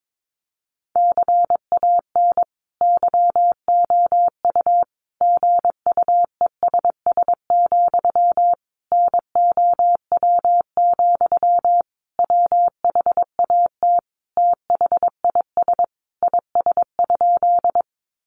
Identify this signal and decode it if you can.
Morse code, heard as a keyed tone, about 22 wpm, 700 Hz. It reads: CAN YOU GUESS, NOW, WHAT THIS IS?
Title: Sentences from Top 1500 Words 22wpm